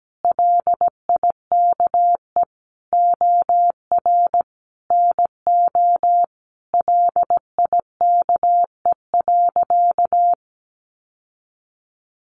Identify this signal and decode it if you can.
Morse code, heard as a keyed tone, about 17 wpm, 700 Hz. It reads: LIKE OR NO LIKE.
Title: Sentences from Top 1000 Words 17wpm